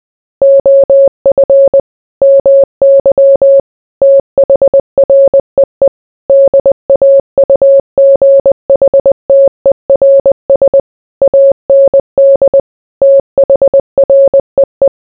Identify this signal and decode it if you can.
Morse code, heard as a keyed tone, about 20 wpm, 550 Hz. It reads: OF MY THREE DAUGHTERS AND THREE